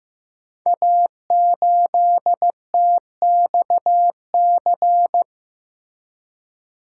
Morse code, heard as a keyed tone, about 15 wpm, 700 Hz. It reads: A8TXC
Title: Code Groups - 5 Letter:Numbers - 15wpm